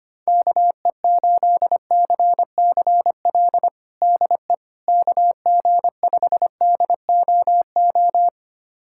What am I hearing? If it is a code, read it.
Morse code, heard as a keyed tone, about 25 wpm, 700 Hz. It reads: KE8CCL DE KG5DOO
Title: USA Callsign DE Callsign - 6 Characters - 25wpm